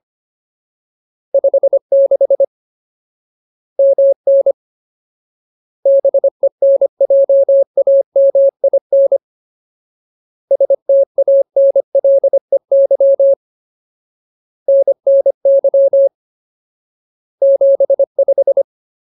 Morse code, heard as a keyed tone, about 25 wpm, 550 Hz. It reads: 56 MN BENJAMIN STANLEY NNY 75
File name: Ultimate Word List / 25WPM Ultimate